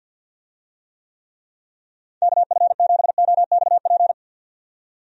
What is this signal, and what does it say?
Morse code, heard as a keyed tone, about 49 wpm, 700 Hz. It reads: KF7OXP
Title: USA Call Signs - 6 Characters - 50wpm